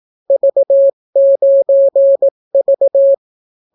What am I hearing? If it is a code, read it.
Morse code, heard as a keyed tone, about 18 wpm, 550 Hz. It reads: V9V